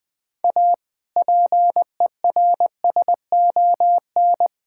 Morse code, heard as a keyed tone, about 20 wpm, 700 Hz. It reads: A PERSON
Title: Sentences with 6-Word Phrases - Spoken Beginning - 20wpm